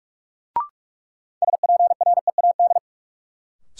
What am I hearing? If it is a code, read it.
Morse code, heard as a keyed tone, about 46 wpm, 700 Hz. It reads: SPREAD
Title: Sentences from Top 1200 Words - New Word Drill - 45wpm